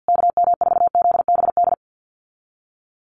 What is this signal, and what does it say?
Morse code, heard as a keyed tone, about 50 wpm, 700 Hz. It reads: KM4ZBD
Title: USA Call Signs Repeated 3x 50wpm